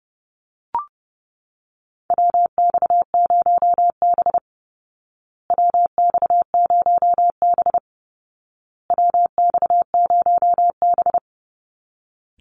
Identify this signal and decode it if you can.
Morse code, heard as a keyed tone, about 30 wpm, 700 Hz. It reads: WX0B WX0B WX0B
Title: Call Signs Repeated 2x - 30wpm